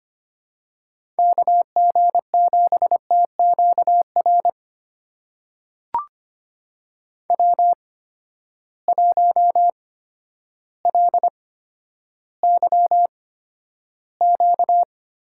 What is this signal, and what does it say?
Morse code, heard as a keyed tone, about 25 wpm, spaced as for 11 wpm, 700 Hz. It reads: KG7TQR W1LYQ
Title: USA Call Signs - 8x Character Spacing - 25wpm